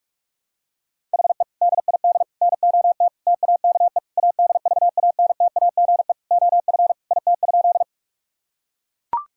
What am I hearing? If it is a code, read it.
Morse code, heard as a keyed tone, about 41 wpm, 700 Hz. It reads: HE DID NOT TAKE ADVANTAGE OF IT?
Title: Sentences from Top 1000 Words - Review Part 6 - 45wpm